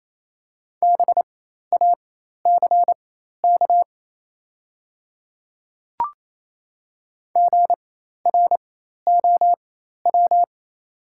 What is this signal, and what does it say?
Morse code, heard as a keyed tone, about 28 wpm, spaced as for 13 wpm, 700 Hz. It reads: BACK GROW
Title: 4x Character Spacing - Top 300 Words - 28wpm